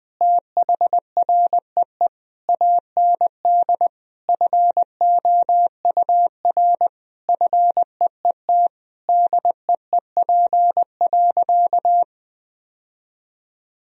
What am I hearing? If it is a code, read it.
Morse code, heard as a keyed tone, about 21 wpm, 700 Hz. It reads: THREE AND FOUR FEET DEEP.
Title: Sentences from Top 1000 Words - Review Part 3 - 20wpm